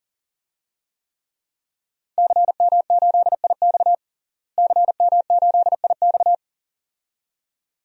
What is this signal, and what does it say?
Morse code, heard as a keyed tone, about 40 wpm, 700 Hz. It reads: CM8IX CM8IX
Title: DXCC-100 Call Signs - 5 Characters - Repeated 2x - 40wpm